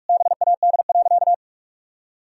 Morse code, heard as a keyed tone, about 42 wpm, 700 Hz. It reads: BAD.
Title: Sentences from Top 1000 Words - Review Part 1 - 45wpm